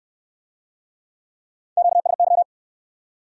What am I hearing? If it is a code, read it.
Morse code, heard as a keyed tone, about 64 wpm, 700 Hz. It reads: QSY